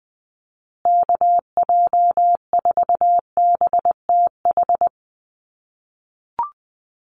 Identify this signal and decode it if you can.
Morse code, heard as a keyed tone, about 20 wpm, 700 Hz. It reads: KJ4BTH
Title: USA Call Signs Repeated 3x 20wpm